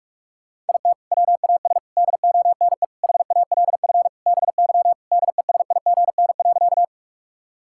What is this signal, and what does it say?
Morse code, heard as a keyed tone, about 41 wpm, 700 Hz. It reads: IT WAS DONE HALF BY DESIGN.
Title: Sentences from Top 1000 Words - Review Part 5 - 45wpm